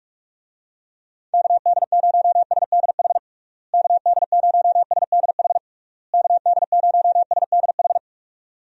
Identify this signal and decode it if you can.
Morse code, heard as a keyed tone, about 46 wpm, 700 Hz. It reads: KD0SDH KD0SDH KD0SDH
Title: USA Call Signs - 6 Characters - Repeated 3x - 45wpm